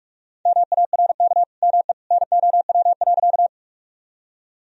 Morse code, heard as a keyed tone, about 46 wpm, 700 Hz. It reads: MARK ME NOW.
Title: Sentences from Top 1000 Words - Review Part 1 - 45wpm